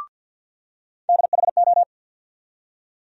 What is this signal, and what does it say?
Morse code, heard as a keyed tone, about 52 wpm, 700 Hz. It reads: DHO
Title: USA Call Sign Postfixes - 3 Letters 50wpm